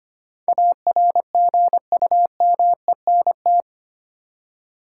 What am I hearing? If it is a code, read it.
Morse code, heard as a keyed tone, about 25 wpm, 700 Hz. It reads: ARGUMENT
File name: Eight Letter Words - 25wpm